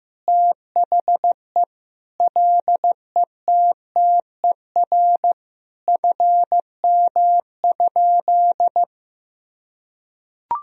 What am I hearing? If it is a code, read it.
Morse code, heard as a keyed tone, about 14 wpm, 700 Hz. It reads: THE LETTER FM?
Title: Sentences from Top 300 Words and Abbreviations 15wpm